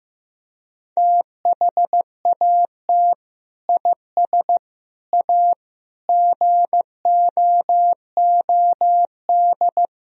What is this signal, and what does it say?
Morse code, heard as a keyed tone, about 15 wpm, 700 Hz. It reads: THAT IS A GOOD
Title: Sentences from Top 1000 Words - Review Part 4 - 15wpm